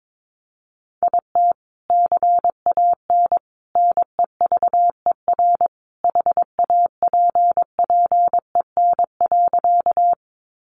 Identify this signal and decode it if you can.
Morse code, heard as a keyed tone, about 22 wpm, 700 Hz. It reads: IT CAN NEVER HAPPEN.
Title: Sentences from Top 400 Words 22wpm